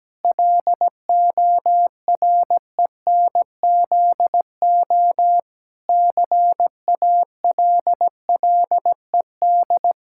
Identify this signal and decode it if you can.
Morse code, heard as a keyed tone, about 17 wpm, 700 Hz. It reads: LORENZO CALLED
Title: Pangram Challenge v2 - 17wpm